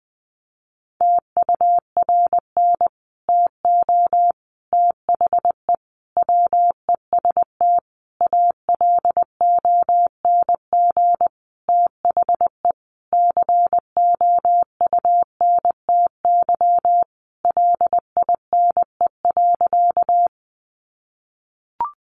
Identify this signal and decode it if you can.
Morse code, heard as a keyed tone, about 20 wpm, 700 Hz. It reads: TURN TO THE WEST ALONG THE COUNTY LINE.
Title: Sentences from Top 1200 Words 20wpm